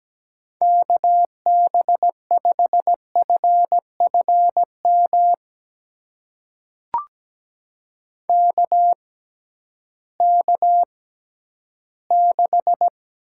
Morse code, heard as a keyed tone, about 17 wpm, spaced as for 11 wpm, 700 Hz. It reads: KB5FFM KK6
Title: USA Call Signs - 6 Characters - 6x Character Spacing - 17wpm